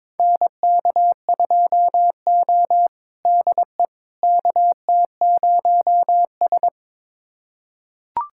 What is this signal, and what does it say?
Morse code, heard as a keyed tone, about 22 wpm, 700 Hz. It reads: NK2O DE KT0S
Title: USA Callsign DE Callsign - 4 Characters - 22wpm